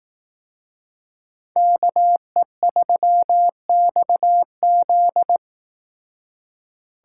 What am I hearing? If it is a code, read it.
Morse code, heard as a keyed tone, about 18 wpm, 700 Hz. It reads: KE3XZ
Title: Speed Racing - USA 2x2 Call Signs 15wpm